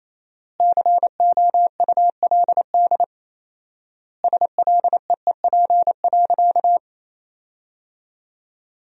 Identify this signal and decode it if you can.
Morse code, heard as a keyed tone, about 28 wpm, 700 Hz. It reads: COULD SLEEP.
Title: 4x Word-Spacing - Sentences from Top 500 Words - 28wpm